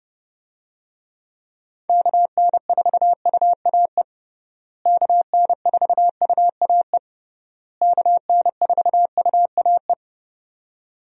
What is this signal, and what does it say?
Morse code, heard as a keyed tone, about 32 wpm, 700 Hz. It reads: KN4UAE KN4UAE KN4UAE
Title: USA Call Signs - 6 Characters - Repeated 3x - 30wpm